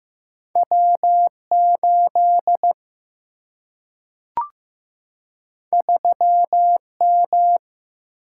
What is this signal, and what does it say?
Morse code, heard as a keyed tone, about 15 wpm, 700 Hz. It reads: W8 3M
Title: Single-Letter Single-Number 15wpm